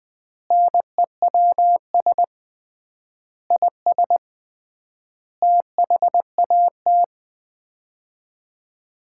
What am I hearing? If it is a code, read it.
Morse code, heard as a keyed tone, about 20 wpm, 700 Hz. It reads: NEWS IS THAT
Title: 3x Word-Spacing - Sets of 5 Words - 20wpm